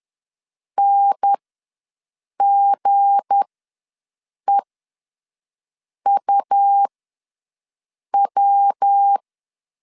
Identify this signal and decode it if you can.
Morse code, heard as a keyed tone, about 10 wpm, spaced as for 5 wpm, 800 Hz. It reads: NGEUW
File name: LEZIONI / 8 YJ45 / 8 YJ45 ws 10wpm 3843